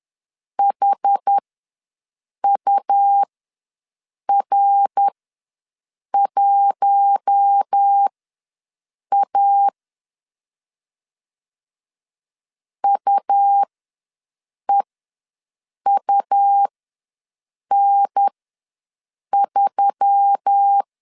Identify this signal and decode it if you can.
Morse code, heard as a keyed tone, about 10 wpm, spaced as for 6 wpm, 800 Hz. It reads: HUR1A UEUN3
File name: LEZIONI / 5 LBVC / 5 LB ws 10wpm 3616